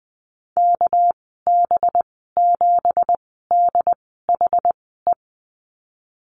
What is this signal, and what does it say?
Morse code, heard as a keyed tone, about 20 wpm, spaced as for 15 wpm, 700 Hz. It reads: KB7DHE
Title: 219 Lesson 35g - Number 0 - Call Signs 20wpm